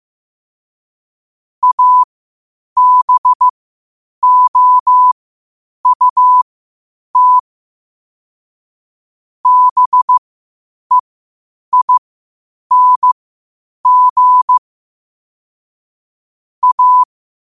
Morse code, heard as a keyed tone, about 15 wpm, spaced as for 8 wpm, 1000 Hz. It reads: ABOUT BEING A
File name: qotdfiles / 8wpm